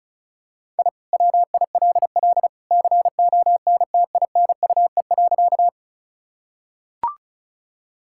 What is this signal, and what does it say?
Morse code, heard as a keyed tone, about 36 wpm, 700 Hz. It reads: I WILL CONTINUE.
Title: Sentences from Top 1000 Words - Review Part 6 - 35wpm